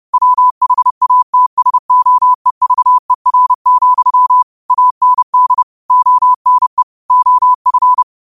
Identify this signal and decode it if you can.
Morse code, heard as a keyed tone, about 30 wpm, 1000 Hz. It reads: WHATSOEVER, AND ONE OF
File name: qotdfiles / 30wpm